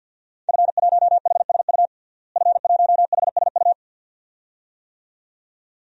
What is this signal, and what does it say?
Morse code, heard as a keyed tone, about 52 wpm, 700 Hz. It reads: F1HSU F1HSU
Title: DXCC-100 Call Signs - 5 Characters - Repeated 2x - 50wpm